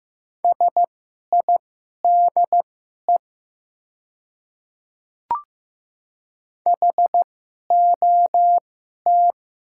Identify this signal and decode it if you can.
Morse code, heard as a keyed tone, about 15 wpm, spaced as for 11 wpm, 700 Hz. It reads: SIDE HOT
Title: 2x Character Spacing - Top 100 Words - 15wpm